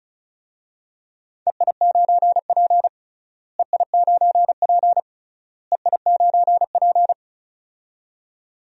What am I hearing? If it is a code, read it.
Morse code, heard as a keyed tone, about 36 wpm, 700 Hz. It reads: EI9P EI9P EI9P
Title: DXCC-100 Call Signs - 4 Characters - Repeated 3x - 35wpm